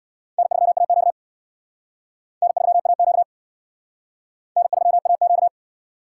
Morse code, heard as a keyed tone, about 67 wpm, 700 Hz. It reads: N3AQ N3AQ N3AQ